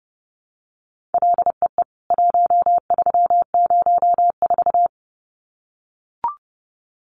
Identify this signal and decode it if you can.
Morse code, heard as a keyed tone, about 28 wpm, 700 Hz. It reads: LEE 1304
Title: CWT Exchange 30wpm